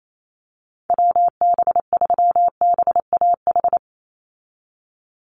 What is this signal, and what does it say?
Morse code, heard as a keyed tone, about 28 wpm, 700 Hz. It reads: WB3BAH
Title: USA Call Signs 28wpm